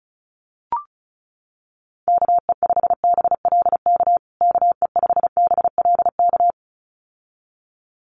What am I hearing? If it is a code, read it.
Morse code, heard as a keyed tone, about 34 wpm, 700 Hz. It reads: KE5BLK KE5BLK
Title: USA Call Signs Repeated 2x 35wpm